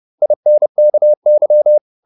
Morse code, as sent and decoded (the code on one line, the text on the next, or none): .. -. -.- -.--
INKY